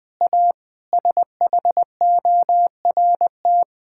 Morse code, heard as a keyed tone, about 20 wpm, 700 Hz. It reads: A SHORT
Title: Difficult Sentences with Hints - 20wpm